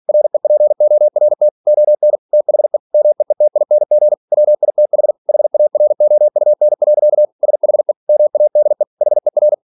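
Morse code, heard as a keyed tone, about 46 wpm, 600 Hz. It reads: REPORT ON THE MEETING WITH HAROUN. SHE MADE HER
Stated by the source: Practice text from an English novel sent at 47 WPM.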